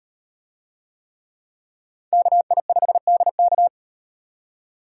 Morse code, heard as a keyed tone, about 38 wpm, 700 Hz. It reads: KI5DK